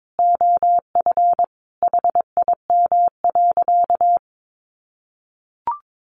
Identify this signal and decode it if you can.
Morse code, heard as a keyed tone, about 22 wpm, 700 Hz. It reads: OF HIM.
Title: Sentences from Top 1100 Words 22wpm